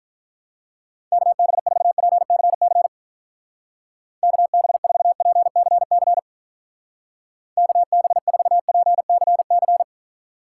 Morse code, heard as a keyed tone, about 45 wpm, 700 Hz. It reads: KB4PCC KB4PCC KB4PCC